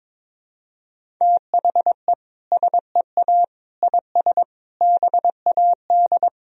Morse code, heard as a keyed tone, about 22 wpm, 700 Hz. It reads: THE SEA IS BAD
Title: Sentences from Top 1000 Words - Review Part 1 - 22wpm